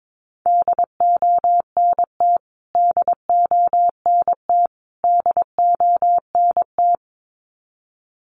Morse code, heard as a keyed tone, about 22 wpm, 700 Hz. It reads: DONT DONT DONT
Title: Top 500 Words Repeated 22wpm